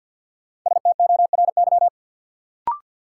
Morse code, heard as a keyed tone, about 53 wpm, 700 Hz. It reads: STORY
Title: Sets of 3 Words - New Word Drill - 50wpm